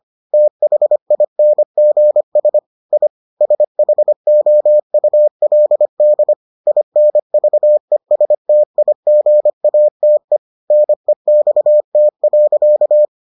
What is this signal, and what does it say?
Morse code, heard as a keyed tone, about 25 wpm, 600 Hz. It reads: THINGS I SHOULD INVESTIGATE NEXT.